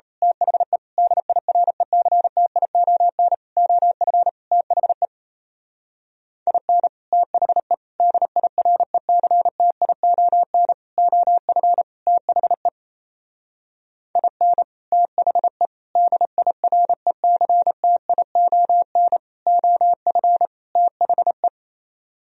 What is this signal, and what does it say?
Morse code, heard as a keyed tone, about 32 wpm, 700 Hz. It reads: THE DIRECTION OF THE IN THE DIRECTION OF THE IN THE DIRECTION OF THE
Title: Speed-Racing - Sets of 5 Words 28wpm